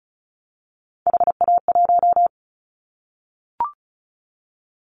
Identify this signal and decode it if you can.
Morse code, heard as a keyed tone, about 34 wpm, 700 Hz. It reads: HA1
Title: Callsign Prefixes for Top 50 DXCC - 35wpm